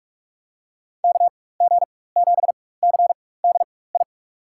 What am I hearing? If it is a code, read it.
Morse code, heard as a keyed tone, about 44 wpm, spaced as for 21 wpm, 700 Hz. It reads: KG7CDI
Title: USA Call Signs - 6 Characters - 4x Character Spacing - 45wpm